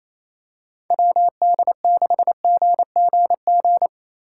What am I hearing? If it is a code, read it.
Morse code, heard as a keyed tone, about 28 wpm, 700 Hz. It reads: WD6GGG
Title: USA Call Signs - 6 Characters - 4x Character Spacing - 28wpm